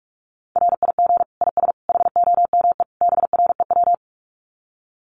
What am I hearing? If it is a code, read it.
Morse code, heard as a keyed tone, about 44 wpm, 700 Hz. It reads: RIG IS HOME BREW
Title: Common QSO Phrases 45wpm